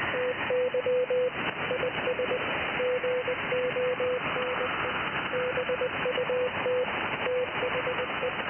Morse code, heard as a keyed tone, about 20 wpm, 500 Hz, with noise. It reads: TY IS GONE BUT THE
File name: IC-7100